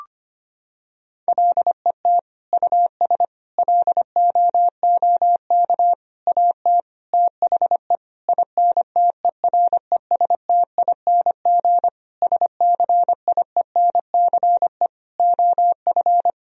Morse code, heard as a keyed tone, about 25 wpm, 700 Hz. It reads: LET US LOOK AT THE INTERESTING SCIENCE OF
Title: Sentences from Top 1000 Words - Review Part 2 - 25wpm